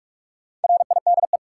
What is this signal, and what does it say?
Morse code, heard as a keyed tone, about 43 wpm, 700 Hz. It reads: RIDE
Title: Sentences from Top 1200 Words - New Word Drill Alt - 45wpm